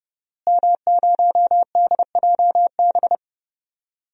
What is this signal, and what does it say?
Morse code, heard as a keyed tone, about 31 wpm, 700 Hz. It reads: M0DJB
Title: DXCC-100 Call Signs - 5 Characters - Repeated 3x - 30wpm